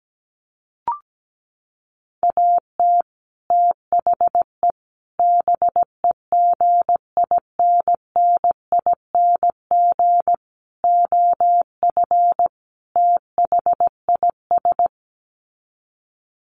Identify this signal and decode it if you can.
Morse code, heard as a keyed tone, about 17 wpm, 700 Hz. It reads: AT THE BEGINNING OF THIS
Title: Sets of 5 Words Encore 17wpm